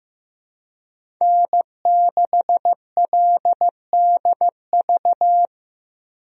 Morse code, heard as a keyed tone, about 15 wpm, 700 Hz. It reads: N6LDV
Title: USA Call Signs - 5 Characters - Repeated 2x - 15wpm